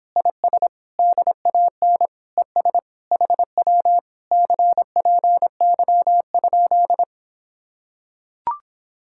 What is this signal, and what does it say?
Morse code, heard as a keyed tone, about 26 wpm, 700 Hz. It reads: IS DAN ES HW CPY?